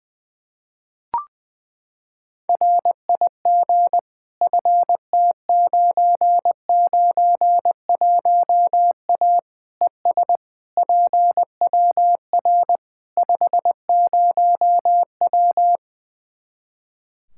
Morse code, heard as a keyed tone, about 20 wpm, 700 Hz. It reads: RIG FT991A ES PWR 50W
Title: K1USN SST QSO Practice 20wpm